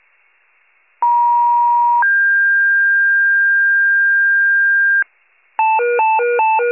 Two-tone page, heard as a one-second tone, then a three-second tone, about 950 then 1600 Hz, with noise. It is heard from 1.0 to 5.0 s.